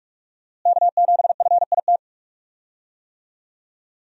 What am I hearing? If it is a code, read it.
Morse code, heard as a keyed tone, about 46 wpm, 700 Hz. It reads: K7FIT
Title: USA Call Signs - 5 Characters - Repeated 3x - 45wpm